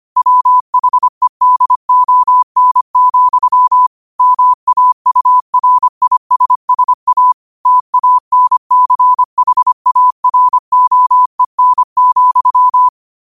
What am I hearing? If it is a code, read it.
Morse code, heard as a keyed tone, about 25 wpm, 1000 Hz. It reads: WHEDON, MAURISSA TANCHAROEN,